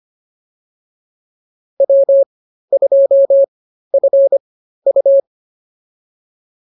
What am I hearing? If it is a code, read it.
Morse code, heard as a keyed tone, about 25 wpm, spaced as for 13 wpm, 550 Hz. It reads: W2FU